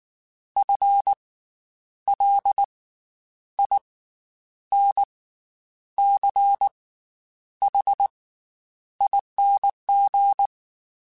Morse code, heard as a keyed tone, about 19 wpm, 800 Hz. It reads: F L I N C H ING
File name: Special INGs / Special INGs-mp3s / Slow ing @18